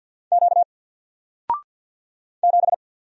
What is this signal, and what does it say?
Morse code, heard as a keyed tone, about 50 wpm, 700 Hz. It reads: Q 7
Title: Single Letter-Number 50wpm